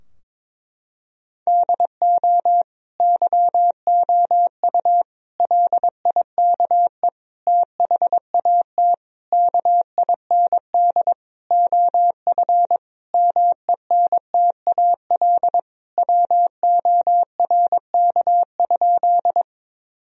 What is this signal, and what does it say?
Morse code, heard as a keyed tone, about 22 wpm, 700 Hz. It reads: DO YOU LIKE THAT KIND OF MENTAL WORK?
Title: Sentences from Top 1000 Words - Review Part 4 - 22wpm